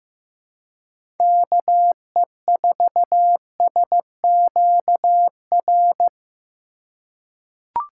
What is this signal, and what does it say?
Morse code, heard as a keyed tone, about 15 wpm, 700 Hz. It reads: KE4SQR
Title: USA Call Signs - 6 Characters - 4x Character Spacing - 15wpm